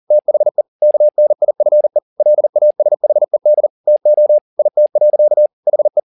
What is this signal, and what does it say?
Morse code, heard as a keyed tone, about 40 wpm, 600 Hz. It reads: THE KNIFE LASHED TO IT. HE
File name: the old man and the sea 2 40wpm